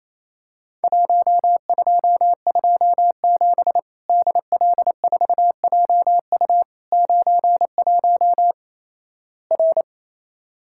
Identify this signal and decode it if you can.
Morse code, heard as a keyed tone, about 28 wpm, 700 Hz. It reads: 1227 DL4JU 91 R
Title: Worked All Europe DX Contest - QTC Exchange - 28wpm